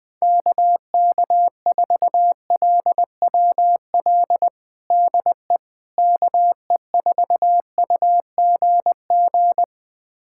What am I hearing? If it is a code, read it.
Morse code, heard as a keyed tone, about 20 wpm, 700 Hz. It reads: KK4LWL DE KE4UGG
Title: USA Callsign DE Callsign - 6 Characters - 20wpm